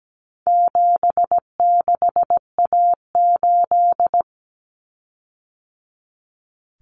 Morse code, heard as a keyed tone, about 17 wpm, 700 Hz. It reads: 76A8
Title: Common Cut Numbers - 4-Digit 17wpm